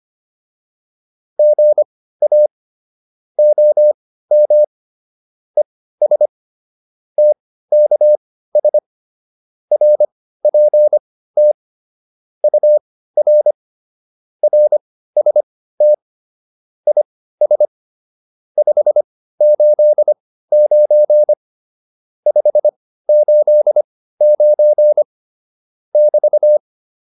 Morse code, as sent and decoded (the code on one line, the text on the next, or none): --. .- / --- -- / . ... / - -.- ... / .-. .--. - / ..- .-. / .-. ... - / .. ... / ..... ---.. ----. / ..... ---.. ----. / -...-
GA OM ES TKS RPT UR RST IS 589 589 <BT>